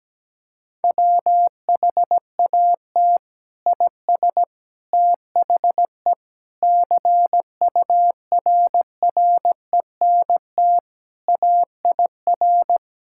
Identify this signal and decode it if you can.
Morse code, heard as a keyed tone, about 17 wpm, 700 Hz. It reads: WHAT IS THE CURRENT AIR
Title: Sentences from Top 1000 Words - Review Part 6 - 17wpm